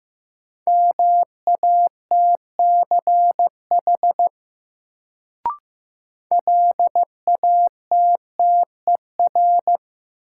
Morse code, heard as a keyed tone, about 15 wpm, 700 Hz. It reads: MATCH LATTER
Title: Sentences from Top 1600 Words - New Word Drill Alt - 15wpm